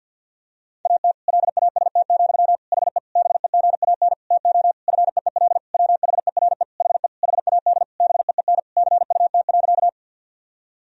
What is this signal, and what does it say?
Morse code, heard as a keyed tone, about 51 wpm, 700 Hz. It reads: AT LAST, HE BEGAN TO FEEL WHERE HE HAD BEEN CUT.
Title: Sentences from Top 1000 Words - Review Part 1 - 50wpm